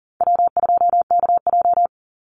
Morse code, heard as a keyed tone, about 40 wpm, 700 Hz. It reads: W2KJ
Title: Easy Call Signs - 40wpm